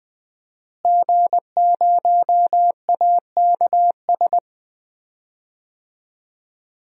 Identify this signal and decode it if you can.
Morse code, heard as a keyed tone, about 20 wpm, 700 Hz. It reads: G0AKS
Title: DXCC-100 Call Signs - 5 Characters - Repeated 2x - 20wpm